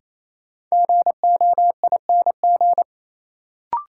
Morse code, heard as a keyed tone, about 28 wpm, 700 Hz. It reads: GOING
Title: Sets of 2 Words - New Word Drill - 28wpm